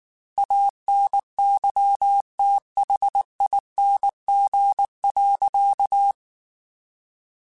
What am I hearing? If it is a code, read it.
Morse code, heard as a keyed tone, about 19 wpm, 800 Hz. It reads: ANYTHING.
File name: Quotes / Quotes-mp3s / Temper @18